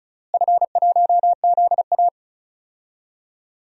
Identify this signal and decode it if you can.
Morse code, heard as a keyed tone, about 34 wpm, 700 Hz. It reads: F1ZA
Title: DXCC-100 Call Signs - 4 Characters - Repeated 3x - 35wpm